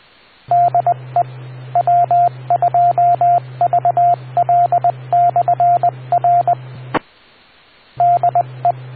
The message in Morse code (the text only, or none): DE W2VL/R DE